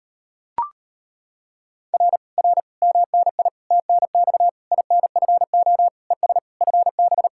Morse code, heard as a keyed tone, about 33 wpm, 700 Hz. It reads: R R MNI TNX INFO ES FB